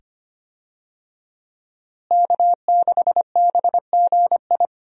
Morse code, heard as a keyed tone, about 25 wpm, 700 Hz. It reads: K6BGI
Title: USA Call Signs - 5 Characters - Repeated 2x - 25wpm